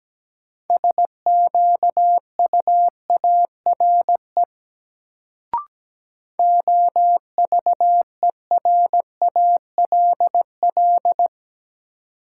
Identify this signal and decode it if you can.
Morse code, heard as a keyed tone, about 17 wpm, 700 Hz. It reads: SQUARE OVERALL
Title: Sentences from Top 1000 Words - Review Part 7 - New Word Drill - 17wpm